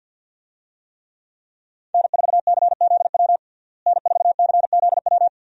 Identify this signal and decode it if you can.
Morse code, heard as a keyed tone, about 53 wpm, 700 Hz. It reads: N4CZW N4CZW
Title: USA Call Signs - 5 Characters - Repeated 2x - 50wpm